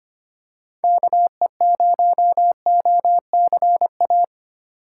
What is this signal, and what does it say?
Morse code, heard as a keyed tone, about 25 wpm, 700 Hz. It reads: KE0OCA
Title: USA Call Signs - 2x Character Spacing - 25wpm